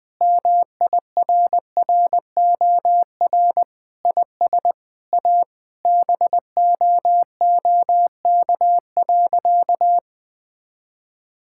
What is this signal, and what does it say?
Morse code, heard as a keyed tone, about 21 wpm, 700 Hz. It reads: MIRROR IS A BOOK.
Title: Sentences with 4-Word Phrases - 20wpm